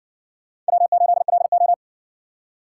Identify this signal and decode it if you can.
Morse code, heard as a keyed tone, about 60 wpm, 700 Hz. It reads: W8LO